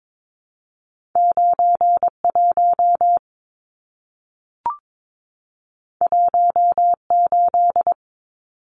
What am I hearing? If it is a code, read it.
Morse code, heard as a keyed tone, about 22 wpm, 700 Hz. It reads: 91 18
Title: Numbers - 2-Digit 22wpm